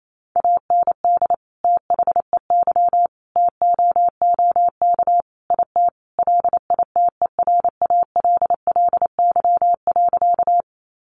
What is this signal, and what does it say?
Morse code, heard as a keyed tone, about 28 wpm, 700 Hz. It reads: AND THEY TOOK IT LITERALLY.
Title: Sentences from Top 1400 Words 28wpm